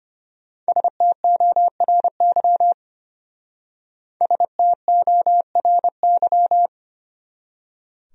Morse code, heard as a keyed tone, about 31 wpm, 700 Hz. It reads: STORY STORY
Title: Speed-Racing Top 500 Words 25wpm